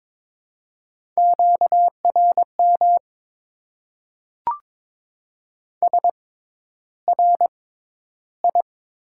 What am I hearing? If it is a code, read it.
Morse code, heard as a keyed tone, about 22 wpm, spaced as for 11 wpm, 700 Hz. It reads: QRM SRI
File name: Top 90 QSO Elements - 6x Character Spacing - 22wpm